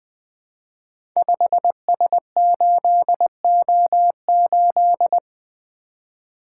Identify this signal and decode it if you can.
Morse code, heard as a keyed tone, about 20 wpm, 700 Hz. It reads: 5S8O8
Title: Code Groups - 5 Letter:Numbers - 20wpm